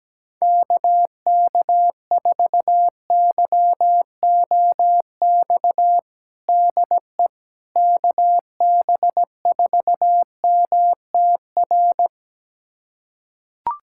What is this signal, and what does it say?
Morse code, heard as a keyed tone, about 17 wpm, 700 Hz. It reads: KK4YOX DE KB4MTR
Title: USA Callsign DE Callsign 17wpm